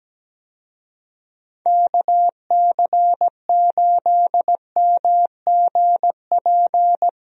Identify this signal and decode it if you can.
Morse code, heard as a keyed tone, about 17 wpm, 700 Hz. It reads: KC8MGP
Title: USA Call Signs - 6 Characters - Repeated 2x - 17wpm